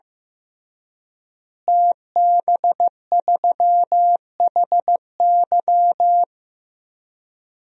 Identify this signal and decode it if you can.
Morse code, heard as a keyed tone, about 15 wpm, 700 Hz. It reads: TB3HY
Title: DXCC-100 Call Signs - 5 Characters - Repeated 3x - 15wpm